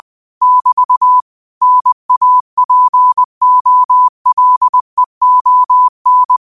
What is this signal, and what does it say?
Morse code, heard as a keyed tone, about 20 wpm, 1000 Hz. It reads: <BT> NAPOLEON
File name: qotdfiles / 20wpm